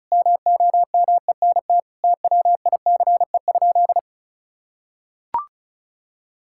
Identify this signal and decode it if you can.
Morse code, heard as a keyed tone, about 34 wpm, 700 Hz. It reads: MOMENT TWICE?
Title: Sentences from Top 1600 Words 35wpm